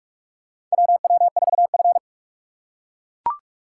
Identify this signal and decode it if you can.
Morse code, heard as a keyed tone, about 45 wpm, 700 Hz. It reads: WW4F
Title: USA Call Signs - 4 Characters - 4x Character Spacing - 45wpm